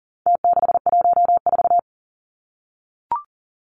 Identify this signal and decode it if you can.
Morse code, heard as a keyed tone, about 40 wpm, 700 Hz. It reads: T614
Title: Numbers - Contest Style - 4-Digit 40wpm